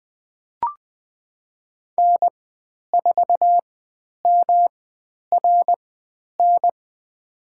Morse code, heard as a keyed tone, about 20 wpm, spaced as for 10 wpm, 700 Hz. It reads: N4MRN
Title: 109 Lesson 18g - Letter m - Call Signs 20wpm